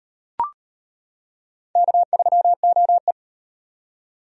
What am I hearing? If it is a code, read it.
Morse code, heard as a keyed tone, about 38 wpm, 700 Hz. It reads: K3OE